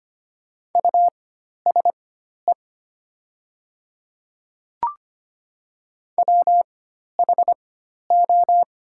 Morse code, heard as a keyed tone, about 25 wpm, spaced as for 12 wpm, 700 Hz. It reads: USE WHO
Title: 4x Character Spacing - Top 100 Words - 25wpm